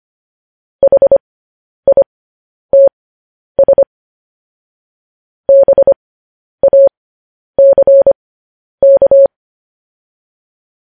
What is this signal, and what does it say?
Morse code, heard as a keyed tone, about 25 wpm, spaced as for 10 wpm, 550 Hz. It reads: HITS BACK
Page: news headlines in morse code at 10 wpm